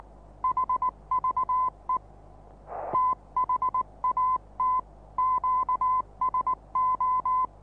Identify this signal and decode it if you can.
Morse code, heard as a keyed tone, about 19 wpm, 1030 Hz. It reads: HVE THAT QSO
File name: repos / Morsekurs / DJ4UF Morsen26